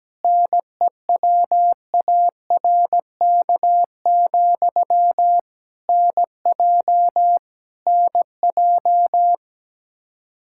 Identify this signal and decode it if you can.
Morse code, heard as a keyed tone, about 17 wpm, 700 Hz. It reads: NEWARK, NJ NJ
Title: K1USN SST QSO Practice 17wpm